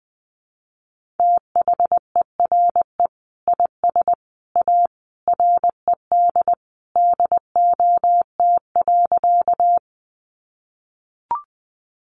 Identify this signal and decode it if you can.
Morse code, heard as a keyed tone, about 20 wpm, 700 Hz. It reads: THERE IS A RED DOT.
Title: Sentences from Top 1500 Words 20wpm